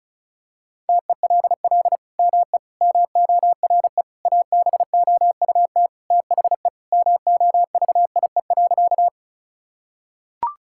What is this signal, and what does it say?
Morse code, heard as a keyed tone, about 36 wpm, 700 Hz. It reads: TELL ME MORE ABOUT THE MOVIE.
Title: Sentences from Top 1000 Words - Review Part 5 - 35wpm